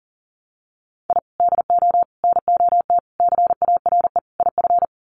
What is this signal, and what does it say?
Morse code, heard as a keyed tone, about 40 wpm, 700 Hz. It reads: I DO NOT CARE IF
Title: Sets of 5 Words 40wpm